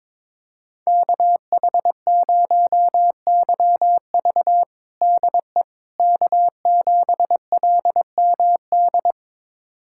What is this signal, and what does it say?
Morse code, heard as a keyed tone, about 21 wpm, 700 Hz. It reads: KH0YV DE K7LMD
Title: USA Callsign DE Callsign - 5 Characters - 22wpm